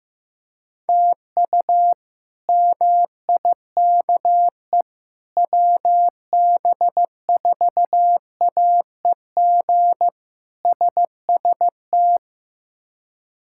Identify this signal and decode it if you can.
Morse code, heard as a keyed tone, about 15 wpm, 700 Hz. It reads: TU MIKE WB4AEG SST
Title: K1USN SST QSO Practice - 15wpm